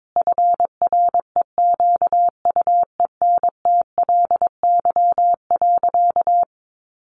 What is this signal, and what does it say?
Morse code, heard as a keyed tone, about 22 wpm, 700 Hz. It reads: FREQUENTLY.
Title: Sentences from Top 1300 Words 22wpm